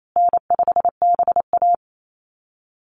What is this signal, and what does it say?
Morse code, heard as a keyed tone, about 28 wpm, 700 Hz. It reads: N5BA
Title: Easy Call Signs Repeated 3x - 28wpm